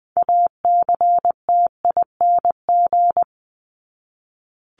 Morse code, heard as a keyed tone, about 20 wpm, 700 Hz. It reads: ACTING
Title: ING Suffix - Top 100 Words with Base Words 20wpm